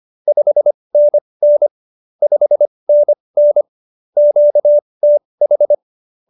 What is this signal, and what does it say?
Morse code, heard as a keyed tone, about 25 wpm, spaced as for 20 wpm, 590 Hz. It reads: 5NN 5NN QTH
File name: QSO301 20